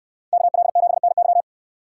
Morse code, heard as a keyed tone, about 66 wpm, 700 Hz. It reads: KL7NY